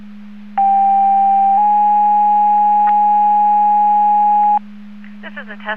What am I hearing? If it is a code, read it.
Two-tone page, heard as a one-second tone, then a three-second tone, about 790 then 830 Hz, with speech. It is heard from 0.6 to 4.6 s.